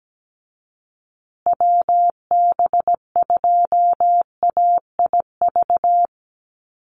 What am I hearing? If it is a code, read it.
Morse code, heard as a keyed tone, about 17 wpm, 700 Hz. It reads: WB2AIV
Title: Call Signs Repeated 3x - 17wpm